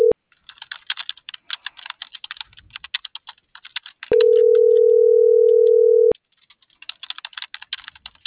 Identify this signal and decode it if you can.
Ringback tone (North America), 440 + 480 Hz.